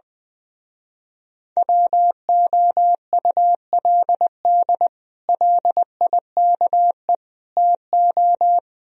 Morse code, heard as a keyed tone, about 20 wpm, 700 Hz. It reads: WOULD LIKE TO
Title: Speed-Racing - Sets of 3 Words 20wpm